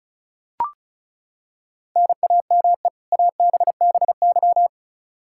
Morse code, heard as a keyed tone, about 34 wpm, 700 Hz. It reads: NAME ABBY
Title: Practice QSO - Exchange 1 - Beginning - 35wpm